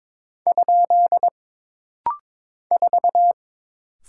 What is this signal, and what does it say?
Morse code, heard as a keyed tone, about 22 wpm, 700 Hz. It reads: ? 4
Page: This rapid-fire timing is geared to push your ICR (Instant Character Recognition) skills.